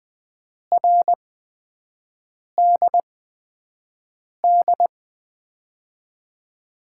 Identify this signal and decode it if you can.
Morse code, heard as a keyed tone, about 20 wpm, spaced as for 5 wpm, 700 Hz. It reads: RDD
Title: USA Call Signs - 6 Characters - 8x Character Spacing - 20wpm